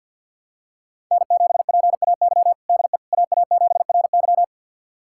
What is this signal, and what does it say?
Morse code, heard as a keyed tone, about 47 wpm, 700 Hz. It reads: N7PAY DE AA7RY
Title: USA Callsign DE Callsign - 5 Characters - 50wpm